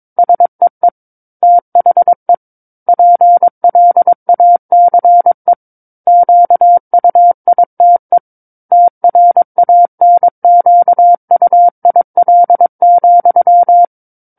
Morse code, heard as a keyed tone, about 22 wpm, 700 Hz. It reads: SEE THE PLACE QUITE TRANQUIL,
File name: Book One / 13-Chapter 13 - 22 WPM